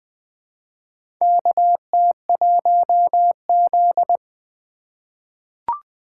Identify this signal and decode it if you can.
Morse code, heard as a keyed tone, about 20 wpm, 700 Hz. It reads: KT1Z
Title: USA Call Signs - 4 Characters - 2x Character Spacing - 20wpm